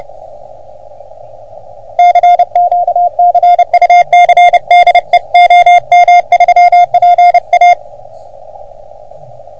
Morse code, heard as a keyed tone, about 30 wpm, 720 Hz. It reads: CQ CUC DE OM3PA
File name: OM3PA PILUP CUC 388